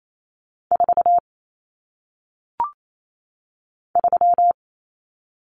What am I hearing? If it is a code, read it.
Morse code, heard as a keyed tone, about 28 wpm, 700 Hz. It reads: 4 3
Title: Numbers - 1-Digit 28wpm